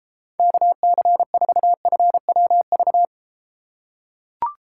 Morse code, heard as a keyed tone, about 32 wpm, 700 Hz. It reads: KC4FWV